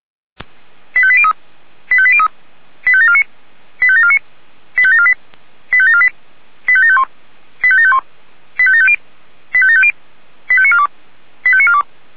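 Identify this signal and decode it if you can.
Five-tone selective call (ZVEI), heard as quick runs of five tones, tones of about 70 ms. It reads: 85803 85803 85749 85749 85748 85748 85751 85751 85750 85750 85942 85942
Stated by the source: ZVEI five-tone call 85863 received via SDR.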